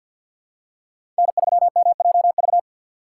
Morse code, heard as a keyed tone, about 53 wpm, 690 Hz. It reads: N3MJV
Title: USA Call Signs - 5 Characters - 2x Character Spacing - 50wpm